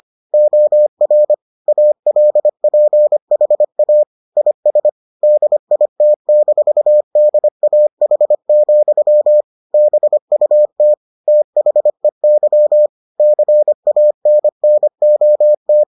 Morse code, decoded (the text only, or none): OR ALPHA IS DIT-DAH, BUT THEY CANNOT